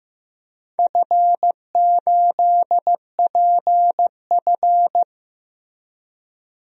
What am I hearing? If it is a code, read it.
Morse code, heard as a keyed tone, about 15 wpm, 700 Hz. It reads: F8PF
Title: DXCC-100 Call Signs - 4 Characters - Repeated 2x - 15wpm